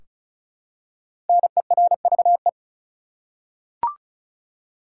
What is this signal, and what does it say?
Morse code, heard as a keyed tone, about 36 wpm, 700 Hz. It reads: NERVE
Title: Five Letter Words - 35wpm